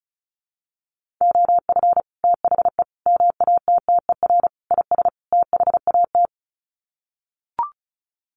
Morse code, heard as a keyed tone, about 34 wpm, 700 Hz. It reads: OF THE MATTER IS THAT
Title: Sets of 5 Words Encore 35wpm